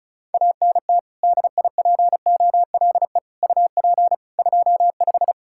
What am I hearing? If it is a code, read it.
Morse code, heard as a keyed tone, about 36 wpm, 700 Hz. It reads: ANT DIPOLE UP 25
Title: K1USN SST QSO Practice 35wpm